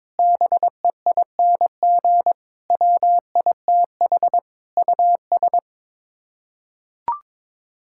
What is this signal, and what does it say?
Morse code, heard as a keyed tone, about 22 wpm, 700 Hz. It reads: BEING WITH US
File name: practice / 22wpm / Build A Phrase - 5-Word Phrases - 22wpm